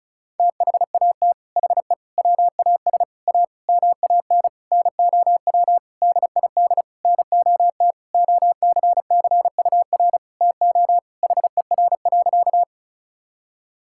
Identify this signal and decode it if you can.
Morse code, heard as a keyed tone, about 36 wpm, 690 Hz. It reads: THAT HE WAS A MAN NOW DID NOT OCCUR TO HER.
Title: Sentences from Top 1000 Words - Review Part 6 - 35wpm